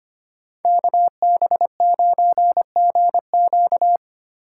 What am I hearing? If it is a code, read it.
Morse code, heard as a keyed tone, about 33 wpm, 700 Hz. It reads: KB9GQ